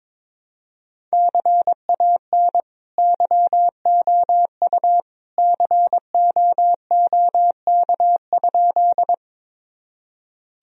Morse code, heard as a keyed tone, about 21 wpm, 700 Hz. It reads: CAN YOU COOK?
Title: Sentences from Top 1000 Words - Review Part 4 - 22wpm